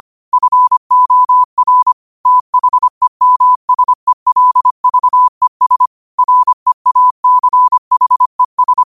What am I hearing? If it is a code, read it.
Morse code, heard as a keyed tone, about 25 wpm, 1000 Hz. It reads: FOR THEMSELVES REACHES